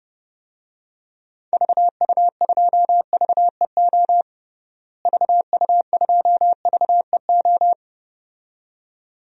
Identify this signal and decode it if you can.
Morse code, heard as a keyed tone, about 30 wpm, 700 Hz. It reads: VU2VEO VU2VEO
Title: DXCC-100 Call Signs - 6 Characters - Repeated 2x - 30wpm